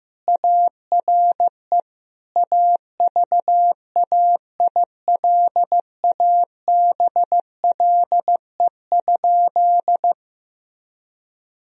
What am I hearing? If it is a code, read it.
Morse code, heard as a keyed tone, about 15 wpm, 700 Hz. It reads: ARE AVAILABLE?
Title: Sentences from Top 1000 Words - Review Part 3 - 15wpm